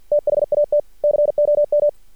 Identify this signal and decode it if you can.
Morse code, heard as a keyed tone, about 50 wpm, 590 Hz. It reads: THAT COM
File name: archived / CRSnet